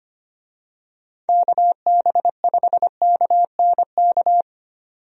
Morse code, heard as a keyed tone, about 25 wpm, 700 Hz. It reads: KB5KNK
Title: USA Call Signs - 6 Characters - 25wpm